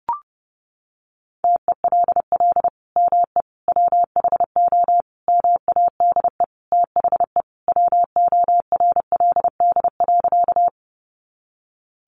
Morse code, heard as a keyed tone, about 30 wpm, 700 Hz. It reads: TELL ME WHO MADE THE WORLD.
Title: Sentences from Top 200 Words 30wpm